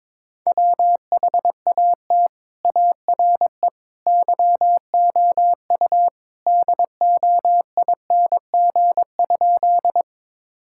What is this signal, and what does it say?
Morse code, heard as a keyed tone, about 22 wpm, 700 Hz. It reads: WHAT ARE YOU DOING?
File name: practice / 22wpm / Sentences with 3-Word Phrases - 22wpm